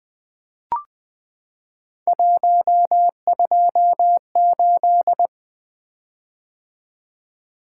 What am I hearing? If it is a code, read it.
Morse code, heard as a keyed tone, about 20 wpm, 700 Hz. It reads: 128
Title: Three Letter-Number Combinations - 20wpm